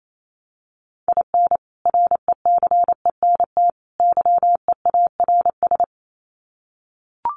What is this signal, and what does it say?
Morse code, heard as a keyed tone, about 28 wpm, 700 Hz. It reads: IN RECENT YEARS
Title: Sets of 3 Words 28wpm